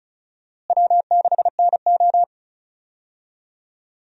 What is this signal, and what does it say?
Morse code, heard as a keyed tone, about 36 wpm, 700 Hz. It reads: W6NO
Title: USA Call Signs - 4 Characters - Repeated 3x - 35wpm